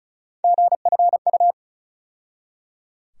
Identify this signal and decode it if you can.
Morse code, heard as a keyed tone, about 36 wpm, 700 Hz. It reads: GFU
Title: USA Call Sign Postfixes - 3 Letters 35wpm